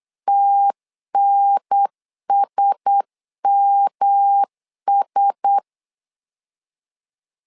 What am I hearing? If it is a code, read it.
Morse code, heard as a keyed tone, about 8 wpm, 800 Hz. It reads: TNSMS
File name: LEZIONI / 4 Q1S 8wpm 2710